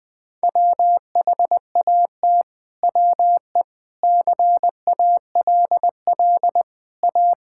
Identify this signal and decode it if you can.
Morse code, heard as a keyed tone, about 20 wpm, 700 Hz. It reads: WHAT WE CALL A
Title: Sentences from Top 1600 Words 20wpm